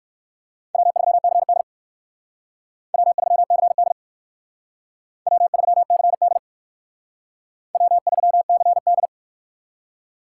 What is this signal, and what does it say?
Morse code, heard as a keyed tone, about 56 wpm, 700 Hz. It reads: W3CD W3CD W3CD W3CD